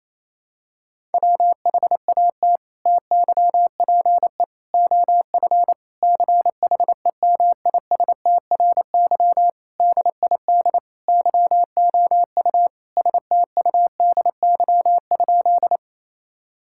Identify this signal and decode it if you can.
Morse code, heard as a keyed tone, about 28 wpm, 700 Hz. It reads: WHAT TYPE OF CHEMISTRY DID YOU STUDY?
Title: Sentences from Top 1000 Words - Review Part 6 - 28wpm